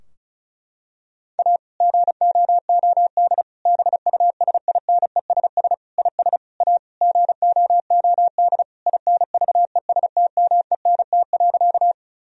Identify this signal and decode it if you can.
Morse code, heard as a keyed tone, about 36 wpm, 700 Hz. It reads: A GOOD BUSINESS IS A GOOD INVESTMENT.
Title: Sentences from Top 1000 Words - Review Part 4 - 35wpm